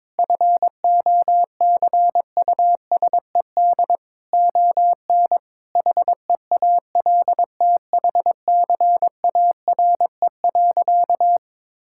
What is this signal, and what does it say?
Morse code, heard as a keyed tone, about 22 wpm, 700 Hz. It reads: FOCUSED ON HEALTHCARE.
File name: Sentences with 7-Word Phrases - 22wpm